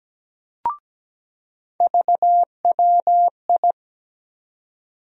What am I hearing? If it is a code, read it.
Morse code, heard as a keyed tone, about 17 wpm, 700 Hz. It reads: VWI
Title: USA Call Sign Postfixes - 3 Letters 17wpm